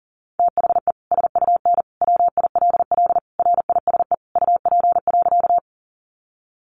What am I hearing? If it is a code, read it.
Morse code, heard as a keyed tone, about 40 wpm, 700 Hz. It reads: THE SUN WILL RISE UP.
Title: Sentences from Top 900 Words 40wpm